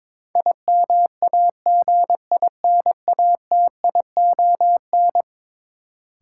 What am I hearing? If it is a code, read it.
Morse code, heard as a keyed tone, about 22 wpm, 700 Hz. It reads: IMAGINATION
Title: Sentences from Top 1300 Words - New Word Drill - 22wpm